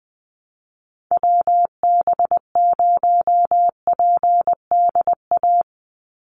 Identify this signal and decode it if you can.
Morse code, heard as a keyed tone, about 20 wpm, 700 Hz. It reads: WB0PDA
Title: USA Call Signs 20wpm